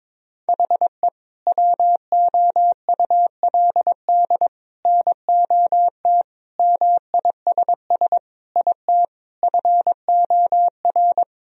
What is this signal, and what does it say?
Morse code, heard as a keyed tone, about 22 wpm, 700 Hz. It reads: HE WOULD NOT MISS IT FOR
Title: Sentences from Top 1000 Words - Review Part 3 - 22wpm